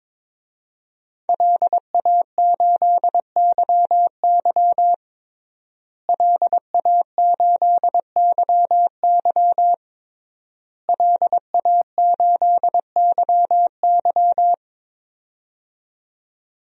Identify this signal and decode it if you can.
Morse code, heard as a keyed tone, about 22 wpm, 700 Hz. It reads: LA8YY LA8YY LA8YY
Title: DXCC-100 Call Signs - 5 Characters - Repeated 3x - 22wpm